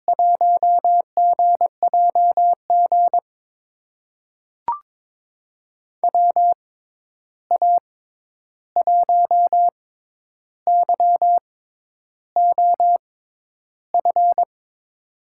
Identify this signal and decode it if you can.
Morse code, heard as a keyed tone, about 22 wpm, spaced as for 10 wpm, 700 Hz. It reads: 1GJG WA1YOF
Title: USA Call Signs - 6x Character Spacing - 22wpm